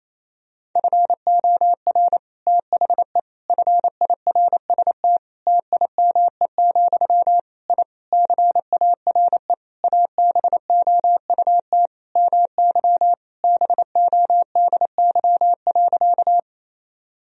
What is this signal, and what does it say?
Morse code, heard as a keyed tone, about 28 wpm, 700 Hz. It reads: FOR THE FIRST TIME, I CARE ABOUT MY BODY.
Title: Sentences from Top 1000 Words - Review Part 2 - 28wpm